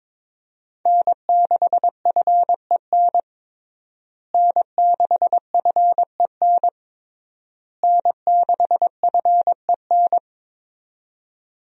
Morse code, heard as a keyed tone, about 22 wpm, 700 Hz. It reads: N6FEN N6FEN N6FEN
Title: USA Call Signs - 5 Characters - Repeated 3x - 22wpm